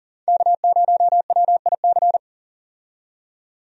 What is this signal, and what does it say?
Morse code, heard as a keyed tone, about 40 wpm, 700 Hz. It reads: K0WIC
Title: USA Call Signs - 5 Characters - 40wpm